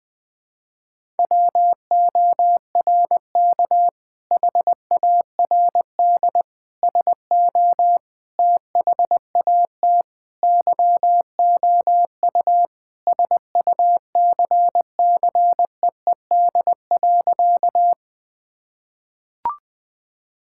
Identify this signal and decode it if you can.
Morse code, heard as a keyed tone, about 21 wpm, 700 Hz. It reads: WORK HARD SO THAT YOU SUCCEED.
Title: Sentences with 2-Word Phrases - 20wpm